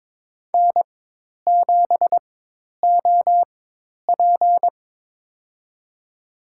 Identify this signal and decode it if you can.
Morse code, heard as a keyed tone, about 22 wpm, spaced as for 10 wpm, 700 Hz. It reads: N7OP
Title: USA Call Signs - 4 Characters - 4x Character Spacing - 22wpm